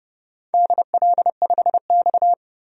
Morse code, heard as a keyed tone, about 30 wpm, 700 Hz. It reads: DL5X
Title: DXCC-100 Call Signs - 4 Characters - 30wpm